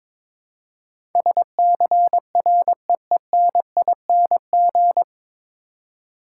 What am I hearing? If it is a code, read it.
Morse code, heard as a keyed tone, about 22 wpm, 700 Hz. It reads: SCREENING
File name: 9 Letter Words - First 3 Letters Given - 22wpm